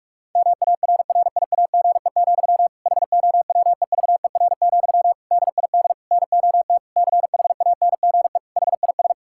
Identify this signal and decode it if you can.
Morse code, heard as a keyed tone, about 46 wpm, 700 Hz. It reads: MARRIAGE, HOWEVER, DID NOT CHANGE HIS
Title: Sentences from Top 1000 Words - Review Part 5 - 45wpm